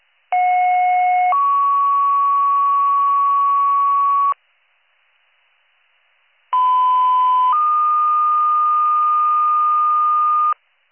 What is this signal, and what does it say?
Two-tone page, heard as a one-second tone, then a three-second tone, about 730 then 1120 Hz, about 1010 then 1220 Hz. It is heard from 0.3 to 10.5 s.